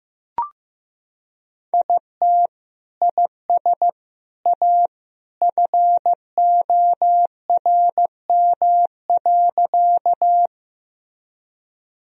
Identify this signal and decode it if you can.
Morse code, heard as a keyed tone, about 15 wpm, 700 Hz. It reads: IT IS A FORM.
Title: Sentences from Top 1000 Words - Review Part 1 - 15wpm